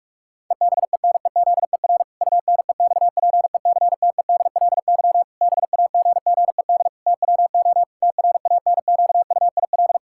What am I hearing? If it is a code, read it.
Morse code, heard as a keyed tone, about 44 wpm, 700 Hz. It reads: EBENEZER UNEXPECTEDLY BAGGED TWO TRANQUIL
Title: Pangram Challenge v2 - 45wpm